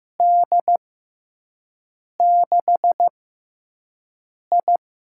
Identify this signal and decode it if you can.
Morse code, heard as a keyed tone, about 15 wpm, spaced as for 5 wpm, 700 Hz. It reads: D6I
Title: USA Call Signs - 4 Characters - 6x Character Spacing - 15wpm